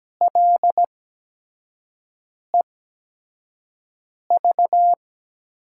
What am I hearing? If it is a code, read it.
Morse code, heard as a keyed tone, about 17 wpm, spaced as for 5 wpm, 700 Hz. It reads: LEV
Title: USA Call Signs - 6 Characters - 8x Character Spacing - 17wpm